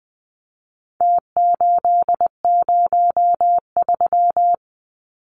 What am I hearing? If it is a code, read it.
Morse code, heard as a keyed tone, about 20 wpm, 700 Hz. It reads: T803
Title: Numbers - Contest Style - 4-Digit 20wpm